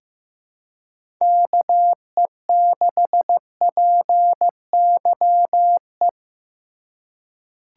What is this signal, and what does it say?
Morse code, heard as a keyed tone, about 16 wpm, 700 Hz. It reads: KE6PYE
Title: USA Call Signs - 6 Characters - 4x Character Spacing - 15wpm